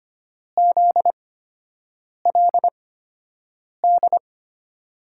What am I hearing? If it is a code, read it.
Morse code, heard as a keyed tone, about 25 wpm, spaced as for 7 wpm, 700 Hz. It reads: ZLD
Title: USA Call Signs - 6 Characters - 8x Character Spacing - 25wpm